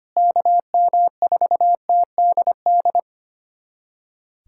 Morse code, heard as a keyed tone, about 25 wpm, 700 Hz. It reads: KM4TDD
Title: USA Callsign DE Callsign - 6 Characters - 25wpm